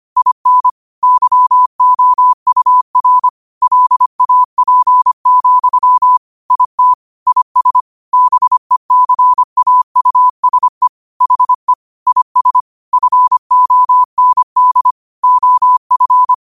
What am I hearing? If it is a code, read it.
Morse code, heard as a keyed tone, about 25 wpm, 1000 Hz. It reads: IN YOUR LAP, IT IS BECAUSE HE IS FOND OF